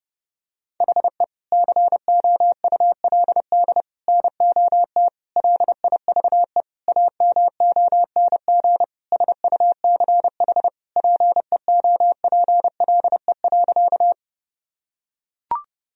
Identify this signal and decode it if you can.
Morse code, heard as a keyed tone, about 30 wpm, 700 Hz. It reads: HE COULD NOT LIVE AMONG SUCH PEOPLE.
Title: Sentences from Top 1000 Words - Review Part 3 - 30wpm